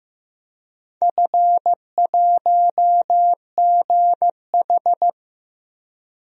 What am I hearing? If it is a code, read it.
Morse code, heard as a keyed tone, about 16 wpm, 700 Hz. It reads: F1GH
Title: DXCC-100 Call Signs - 4 Characters - Repeated 2x - 15wpm